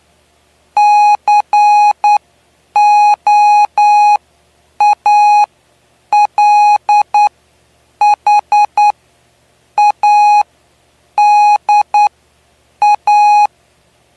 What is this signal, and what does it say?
Morse code, heard as a keyed tone, about 10 wpm, spaced as for 7 wpm, 830 Hz. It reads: COALHADA
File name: cw / py2lcd / 09-PY2LCD-Li��o 9